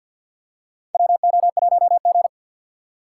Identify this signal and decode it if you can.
Morse code, heard as a keyed tone, about 52 wpm, 700 Hz. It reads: WO1G
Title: USA Call Signs - 6x Character Spacing - 50wpm